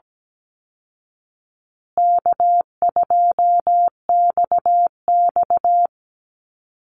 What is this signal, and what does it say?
Morse code, heard as a keyed tone, about 17 wpm, 700 Hz. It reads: K2XX
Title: Call Signs Repeated 3x - 17wpm